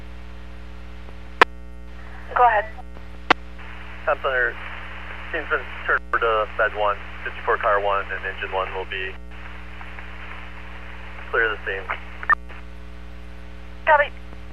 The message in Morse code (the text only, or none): none